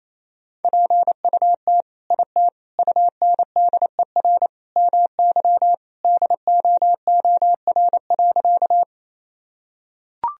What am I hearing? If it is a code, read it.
Morse code, heard as a keyed tone, about 28 wpm, 700 Hz. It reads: PUT IT UNDER MY DOOR.
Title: Sentences from Top 1000 Words - Review Part 1 - 28wpm